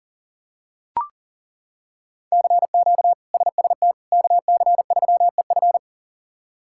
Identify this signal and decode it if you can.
Morse code, heard as a keyed tone, about 40 wpm, 700 Hz. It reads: CQ SST KC3EF
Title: K1USN SST QSO Practice - 40wpm